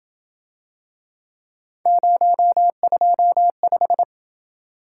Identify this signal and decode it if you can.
Morse code, heard as a keyed tone, about 28 wpm, 700 Hz. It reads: 025